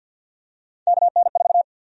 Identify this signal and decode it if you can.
Morse code, heard as a keyed tone, about 48 wpm, 700 Hz. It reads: KN4
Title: USA Call Sign Prefixes - 50wpm